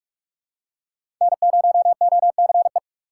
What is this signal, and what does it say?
Morse code, heard as a keyed tone, about 47 wpm, 700 Hz. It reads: N0OCE
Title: USA Call Signs - 4x Character Spacing - 45wpm